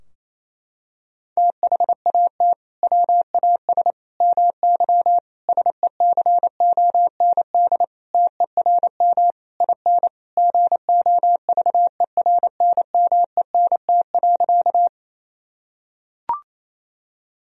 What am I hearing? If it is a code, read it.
Morse code, heard as a keyed tone, about 28 wpm, 700 Hz. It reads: THAT WAS MY SECOND TERM IN GOVERNMENT.
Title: Sentences from Top 1000 Words - Review Part 4 - 28wpm